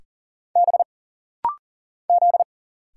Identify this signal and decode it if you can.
Morse code, heard as a keyed tone, about 39 wpm, 700 Hz. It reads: B Z
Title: Single Letter-Number Alt - Rapid-Fire 40wpm